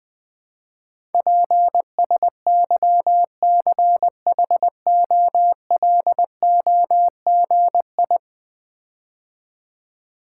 Morse code, heard as a keyed tone, about 20 wpm, 700 Hz. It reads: PSYCHOLOGI
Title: Build A Word - 5 Letters At A Time - 12-Letter Words - 20wpm